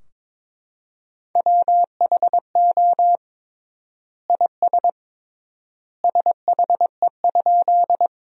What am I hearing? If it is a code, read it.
Morse code, heard as a keyed tone, about 22 wpm, 700 Hz. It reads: WHO IS SHE?
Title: 3x Word-Spacing - Sentences from Top 100 Words - 22wpm